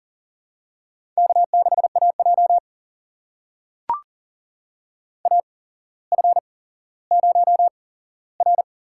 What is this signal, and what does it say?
Morse code, heard as a keyed tone, about 42 wpm, spaced as for 17 wpm, 700 Hz. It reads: K6AJ AF0R
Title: USA Call Signs - 4 Characters - 8x Character Spacing - 40wpm